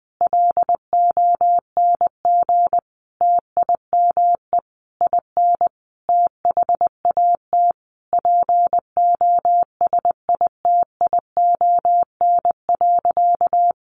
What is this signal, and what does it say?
Morse code, heard as a keyed tone, about 19 wpm, 700 Hz. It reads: LONG TIME IN THAT POSITION.
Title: Sentences from Top 900 Words 20wpm